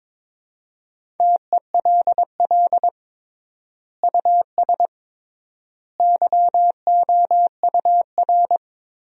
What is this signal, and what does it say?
Morse code, heard as a keyed tone, about 22 wpm, 700 Hz. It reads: TELL US YOUR
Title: 3x Word-Spacing - Sentences from Top 200 Words - 22wpm